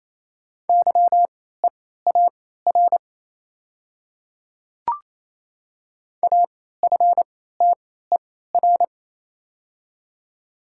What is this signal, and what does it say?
Morse code, heard as a keyed tone, about 28 wpm, spaced as for 16 wpm, 700 Hz. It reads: YEAR AFTER
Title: 3x Character Spacing - Top 200 Words - 28wpm